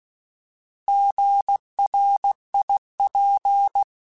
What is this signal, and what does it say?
Morse code, heard as a keyed tone, about 16 wpm, 800 Hz. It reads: GRIP